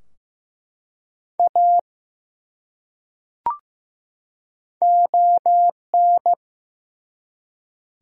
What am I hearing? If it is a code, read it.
Morse code, heard as a keyed tone, about 15 wpm, 700 Hz. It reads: A ON
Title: Top 100 Words 15wpm